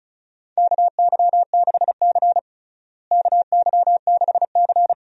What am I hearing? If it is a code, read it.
Morse code, heard as a keyed tone, about 36 wpm, 700 Hz. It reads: KY6C KY6C
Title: USA Call Signs - 4 Characters - Repeated 2x - 35wpm